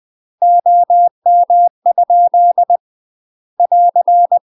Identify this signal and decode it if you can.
Morse code, heard as a keyed tone, about 20 wpm, 700 Hz. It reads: OM? <AR>
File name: UK MASTER QSOs TRAINING / 20WPM / UK MASTER QSOs 0063